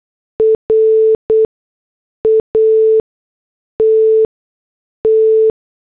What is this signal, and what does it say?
Morse code, heard as a keyed tone, about 8 wpm, spaced as for 6 wpm, 440 Hz. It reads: RATT